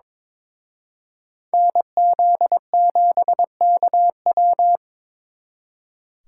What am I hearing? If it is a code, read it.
Morse code, heard as a keyed tone, about 26 wpm, 700 Hz. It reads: NZ7KW
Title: Speed Racing - USA 2x2 Call Signs 22wpm